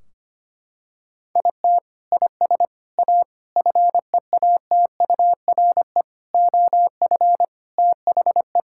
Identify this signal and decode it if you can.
Morse code, heard as a keyed tone, about 25 wpm, 700 Hz. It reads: IT IS A FEATURE OF THE
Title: Sentences from Top 1600 Words 25wpm